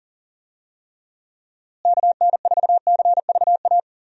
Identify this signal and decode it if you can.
Morse code, heard as a keyed tone, about 40 wpm, 700 Hz. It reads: KN4CVA
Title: USA Call Signs - 6 Characters - Repeated 2x - 40wpm